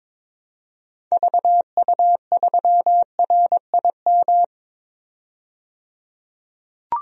Morse code, heard as a keyed tone, about 21 wpm, 700 Hz. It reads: VU3RIM
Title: DXCC-100 Call Signs - 6 Characters - 22wpm